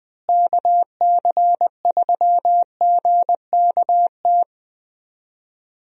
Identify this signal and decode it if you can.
Morse code, heard as a keyed tone, about 20 wpm, 700 Hz. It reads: KC3GKT
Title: USA Call Signs - 6 Characters - Repeated 3x - 20wpm